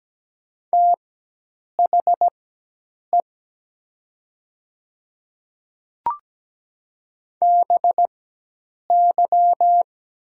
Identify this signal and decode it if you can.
Morse code, heard as a keyed tone, about 17 wpm, spaced as for 8 wpm, 700 Hz. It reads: THE BY
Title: 4x Character Spacing - Top 100 Words - 17wpm